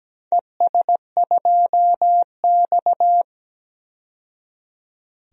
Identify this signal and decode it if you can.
Morse code, heard as a keyed tone, about 17 wpm, 700 Hz. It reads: ES2X
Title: DXCC-100 Call Signs - 4 Characters - 17wpm